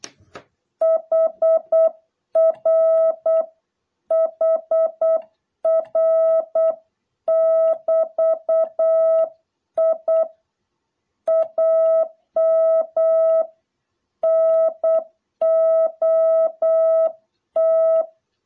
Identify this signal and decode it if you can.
Morse code, heard as a keyed tone, about 8 wpm, 660 Hz. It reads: HR HR<BT>I AM NOT